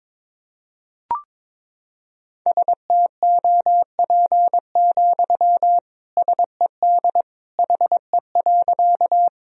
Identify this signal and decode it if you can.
Morse code, heard as a keyed tone, about 21 wpm, 700 Hz. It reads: STOP, SED HE.
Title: Sentences from Top 300 Words and Abbreviations 22wpm